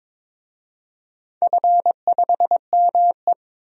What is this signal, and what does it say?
Morse code, heard as a keyed tone, about 22 wpm, 700 Hz. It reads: F5ME
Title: DXCC-100 Call Signs - 4 Characters - 22wpm